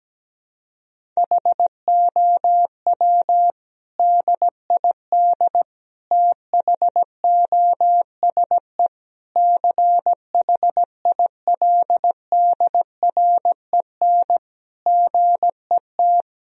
Morse code, heard as a keyed tone, about 17 wpm, 700 Hz. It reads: HOW DID THOSE CHILDREN GET
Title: Sentences from Top 1000 Words - Review Part 1 - 17wpm